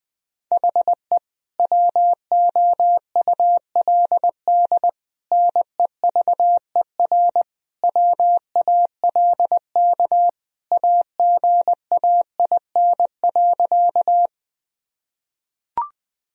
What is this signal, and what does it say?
Morse code, heard as a keyed tone, about 21 wpm, 700 Hz. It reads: HE WOULD NEVER WALK AGAIN.
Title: Sentences from Top 1000 Words - Review Part 1 - 20wpm